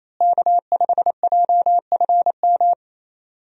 Morse code, heard as a keyed tone, about 28 wpm, 700 Hz. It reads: K5JFM
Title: USA Call Signs - 5 Characters - Repeated 3x - 28wpm